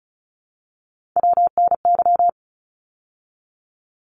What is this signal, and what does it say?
Morse code, heard as a keyed tone, about 36 wpm, 700 Hz. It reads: WNY
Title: ARRL Sections 35wpm